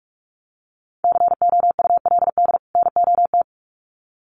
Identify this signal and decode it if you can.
Morse code, heard as a keyed tone, about 46 wpm, 700 Hz. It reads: COULD NOT
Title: Sets of 2 Words 45wpm